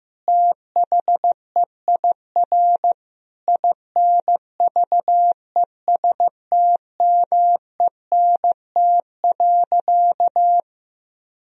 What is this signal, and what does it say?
Morse code, heard as a keyed tone, about 15 wpm, 700 Hz. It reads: THEIR INVESTMENT.
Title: Sentences from Top 1000 Words - Review Part 4 - 15wpm